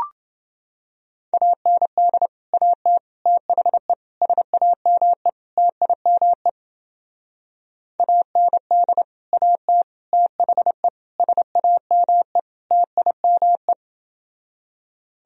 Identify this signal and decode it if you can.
Morse code, heard as a keyed tone, about 30 wpm, 700 Hz. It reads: AND AT THE SAME TIME AND AT THE SAME TIME